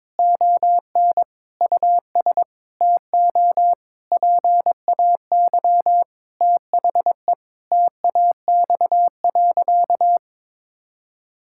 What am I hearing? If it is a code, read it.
Morse code, heard as a keyed tone, about 22 wpm, 700 Hz. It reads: ON US TO PAY THE TAX.
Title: Sentences from Top 1000 Words - Review Part 6 - 22wpm